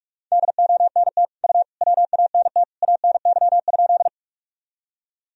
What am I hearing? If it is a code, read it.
Morse code, heard as a keyed tone, about 46 wpm, 700 Hz. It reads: DONT U WANT ANY?
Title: Sentences from Top 300 Words and Abbreviations 45wpm